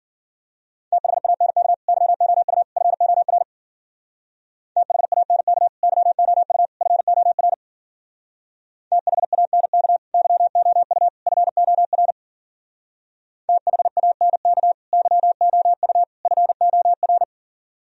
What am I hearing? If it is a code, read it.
Morse code, heard as a keyed tone, about 47 wpm, 700 Hz. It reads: THANK YOU FOR THANK YOU FOR THANK YOU FOR THANK YOU FOR
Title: Speed-Racing - Sets of 3 Words 40wpm